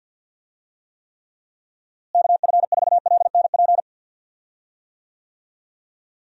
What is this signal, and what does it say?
Morse code, heard as a keyed tone, about 51 wpm, 700 Hz. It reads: KF4LNP
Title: USA Call Signs - 6 Characters - Repeated 3x - 50wpm